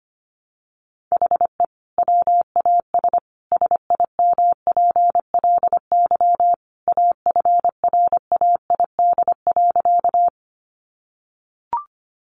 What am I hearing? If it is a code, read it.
Morse code, heard as a keyed tone, about 25 wpm, 700 Hz. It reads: HE WAS SIMPLY AFRAID.
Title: Sentences from Top 1500 Words 25wpm